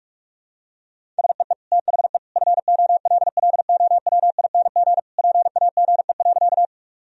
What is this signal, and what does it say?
Morse code, heard as a keyed tone, about 46 wpm, 700 Hz. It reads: SEE THE FOLLOWING PAGE.
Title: Sentences from Top 1000 Words - Review Part 1 - 45wpm